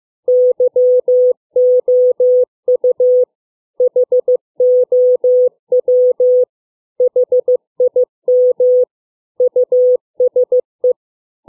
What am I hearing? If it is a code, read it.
Morse code, heard as a keyed tone, about 15 wpm, 500 Hz. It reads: YOU HOW HIM USE
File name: CW / fixed length words 4